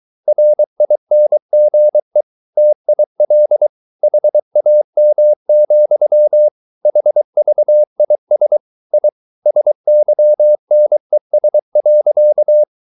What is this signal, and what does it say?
Morse code, heard as a keyed tone, about 23 wpm, 600 Hz. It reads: RINGE TIL HAM, HVIS I SYNES.
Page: Ferie 23 wpm